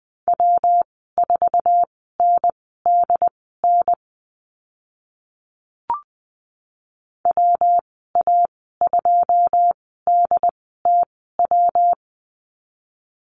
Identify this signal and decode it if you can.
Morse code, heard as a keyed tone, about 20 wpm, spaced as for 14 wpm, 700 Hz. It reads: W4NDN WA2DTW
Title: 122 Lesson 20g - Number 3 - Call Signs 20wpm